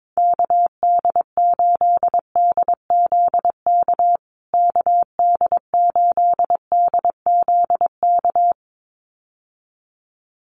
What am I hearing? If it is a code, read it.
Morse code, heard as a keyed tone, about 22 wpm, 700 Hz. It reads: KD8DZK KD8DZK
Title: USA Call Signs Repeated 2x 22wpm